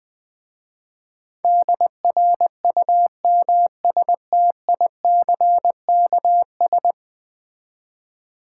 Morse code, heard as a keyed tone, about 20 wpm, 700 Hz. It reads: DRUMSTICKS
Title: Easy Words with Hints - 20wpm